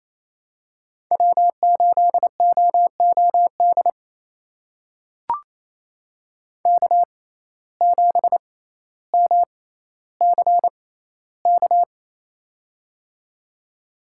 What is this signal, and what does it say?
Morse code, heard as a keyed tone, about 28 wpm, spaced as for 14 wpm, 700 Hz. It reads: W8OOD K7MCK
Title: USA Call Signs - 5 Characters - 6x Character Spacing - 28wpm